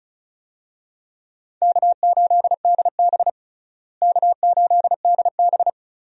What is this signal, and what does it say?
Morse code, heard as a keyed tone, about 36 wpm, 700 Hz. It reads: K8DB K8DB
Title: USA Call Signs - 4 Characters - Repeated 2x - 35wpm